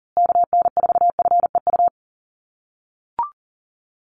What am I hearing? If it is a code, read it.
Morse code, heard as a keyed tone, about 40 wpm, 700 Hz. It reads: KN4FEU
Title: USA Call Signs Repeated 2x 40wpm